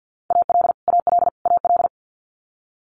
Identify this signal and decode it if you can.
Morse code, heard as a keyed tone, about 50 wpm, 700 Hz. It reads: AL AL AL
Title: Multi-letter Phonemes Repeated 3x 50wpm